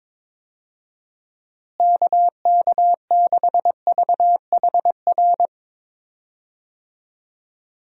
Morse code, heard as a keyed tone, about 22 wpm, 700 Hz. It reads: KK6VHR
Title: USA Call Signs - 6 Characters - Repeated 3x - 22wpm